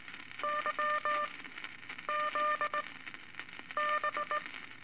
Morse code, heard as a keyed tone, about 19 wpm, spaced as for 8 wpm, 1200 Hz, with noise. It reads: YZB
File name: Morse Code Records / 17A LCM 162 Eight Words Per Minute